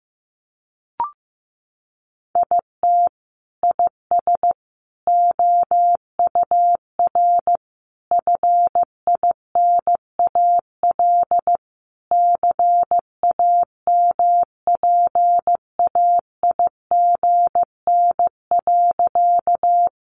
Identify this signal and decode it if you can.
Morse code, heard as a keyed tone, about 15 wpm, 700 Hz. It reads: IT IS OUR FINAL CAMPAIGN.
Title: Sentences from Top 1200 Words 15wpm